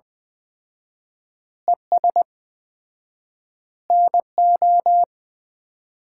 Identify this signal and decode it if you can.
Morse code, heard as a keyed tone, about 20 wpm, 700 Hz. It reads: ES NO
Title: Common QSO Phrases - 4x Word-Spacing - 20wpm